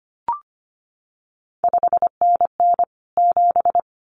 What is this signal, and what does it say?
Morse code, heard as a keyed tone, about 25 wpm, 700 Hz. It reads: 5NN 7
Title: CQ WW DX Contest Exchange - 25wpm